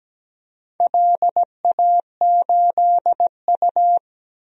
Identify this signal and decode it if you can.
Morse code, heard as a keyed tone, about 17 wpm, 700 Hz. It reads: LA8U
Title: DXCC-100 Call Signs - 4 Characters - 17wpm